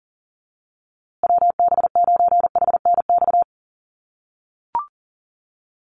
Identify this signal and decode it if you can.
Morse code, heard as a keyed tone, about 40 wpm, 700 Hz. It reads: WB9HNX
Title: USA Call Signs Repeated 3x 40wpm